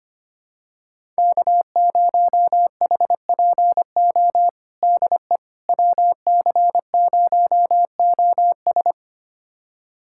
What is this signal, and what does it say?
Morse code, heard as a keyed tone, about 25 wpm, 700 Hz. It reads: K0HPO DE WC0OS
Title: USA Callsign DE Callsign - 5 Characters - 25wpm